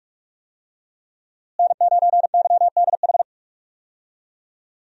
Morse code, heard as a keyed tone, about 44 wpm, 700 Hz. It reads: N9YDH
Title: USA Call Signs - 5 Characters - Repeated 3x - 45wpm